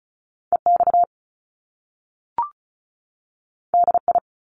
Morse code, heard as a keyed tone, about 36 wpm, 700 Hz. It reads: EX DI
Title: Top Two Letter Combinations in Words 35wpm